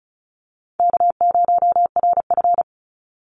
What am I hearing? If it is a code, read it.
Morse code, heard as a keyed tone, about 34 wpm, 700 Hz. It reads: K0RF
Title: Call Signs - 35wpm